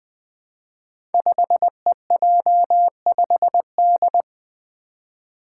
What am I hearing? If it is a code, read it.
Morse code, heard as a keyed tone, about 20 wpm, 700 Hz. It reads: 5EJ5D
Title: Five Challenging Letter-Number Combinations 20wpm